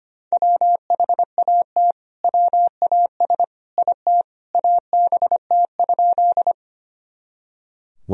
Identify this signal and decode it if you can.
Morse code, heard as a keyed tone, about 25 wpm, 700 Hz. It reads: WHAT WAS IT ABT?
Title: Sentences from Top 100 Words and Abbreviations 25wpm